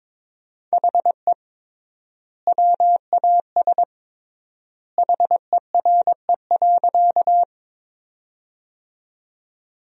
Morse code, heard as a keyed tone, about 21 wpm, 700 Hz. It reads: HE WAS HERE.
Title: 3x Word-Spacing - Sentences from Top 200 Words - 22wpm